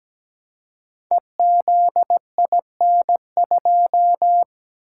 Morse code, heard as a keyed tone, about 17 wpm, 700 Hz. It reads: EZIN2
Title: Code Groups - 5 Letter:Numbers - 17wpm